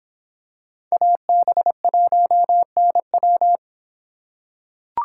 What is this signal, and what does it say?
Morse code, heard as a keyed tone, about 26 wpm, 700 Hz. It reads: AB1NW